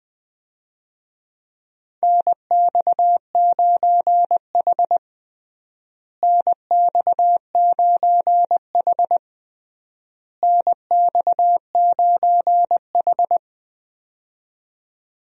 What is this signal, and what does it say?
Morse code, heard as a keyed tone, about 20 wpm, 700 Hz. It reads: NX9H NX9H NX9H
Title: USA Call Signs - 4 Characters - Repeated 3x - 20wpm